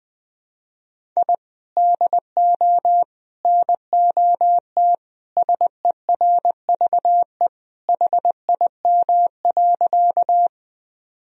Morse code, heard as a keyed tone, about 21 wpm, 700 Hz. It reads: I DO NOT SERVE HIM.
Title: Sentences from Top 1000 Words - Review Part 2 - 20wpm